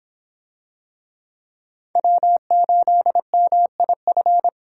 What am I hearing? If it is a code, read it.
Morse code, heard as a keyed tone, about 26 wpm, 700 Hz. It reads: W8MIF